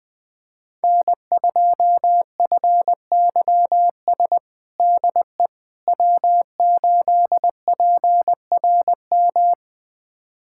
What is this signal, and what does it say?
Morse code, heard as a keyed tone, about 20 wpm, 700 Hz. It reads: N2FYS DE W8PRM
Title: USA Callsign DE Callsign - 5 Characters - 20wpm